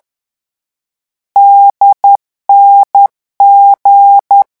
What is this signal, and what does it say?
Morse code, heard as a keyed tone, about 10 wpm, 800 Hz. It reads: DNG
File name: morse mp3 / lecon4-vitesse 10